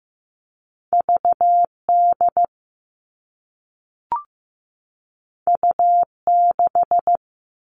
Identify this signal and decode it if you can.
Morse code, heard as a keyed tone, about 15 wpm, 700 Hz. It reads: VD U6
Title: All Cut Numbers - 2-Digit 15wpm